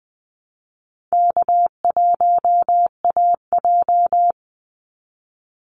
Morse code, heard as a keyed tone, about 20 wpm, 700 Hz. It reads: K1AJ
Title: Call Signs Repeated 3x - 20wpm